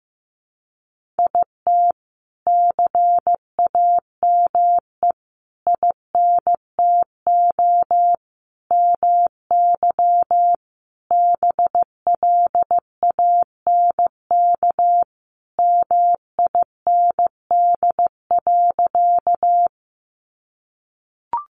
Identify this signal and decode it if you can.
Morse code, heard as a keyed tone, about 15 wpm, 700 Hz. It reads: IT CAME INTO MY BLANK MIND.
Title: Sentences from Top 1500 Words 15wpm